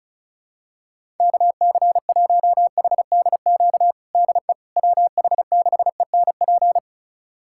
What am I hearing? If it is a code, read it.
Morse code, heard as a keyed tone, about 34 wpm, 700 Hz. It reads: KC1HDQ DE WH6ENP
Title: USA Callsign DE Callsign - 6 Characters - 35wpm